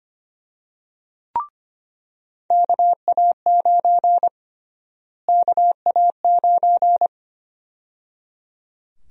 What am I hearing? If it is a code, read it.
Morse code, heard as a keyed tone, about 25 wpm, 700 Hz. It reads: KA9 KA9
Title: USA Call Sign Prefixes - Repeated 2x - 25wpm